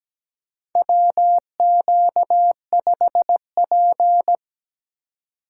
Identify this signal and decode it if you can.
Morse code, heard as a keyed tone, about 17 wpm, 700 Hz. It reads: WQ5P
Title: USA Call Signs - 4 Characters - 4x Character Spacing - 17wpm